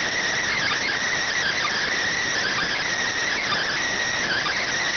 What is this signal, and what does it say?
Emergency Alert System (SAME) tones (header bursts), heard as harsh buzzing bursts. It is heard from 0.4 to 4.7 s.
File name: findings / MR AXIAL 33,99mm 1.gen-44 rx